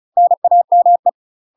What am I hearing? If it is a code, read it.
Morse code, heard as a keyed tone, about 36 wpm, 700 Hz. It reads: NAME
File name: UK MASTER QSOs TRAINING / 35WPM / UK MASTER QSOs 0035